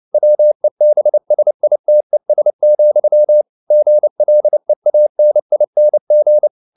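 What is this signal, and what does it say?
Morse code, heard as a keyed tone, about 29 wpm, 600 Hz. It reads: WEBSITES, GLEANING
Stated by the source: Morse code at 29 wpm, sending Ken Follett's novel Never in English, from CW med Gnister.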